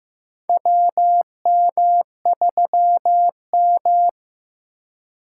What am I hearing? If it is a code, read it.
Morse code, heard as a keyed tone, about 15 wpm, 700 Hz. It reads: WM3M
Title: USA Call Signs - 4x Character Spacing - 15wpm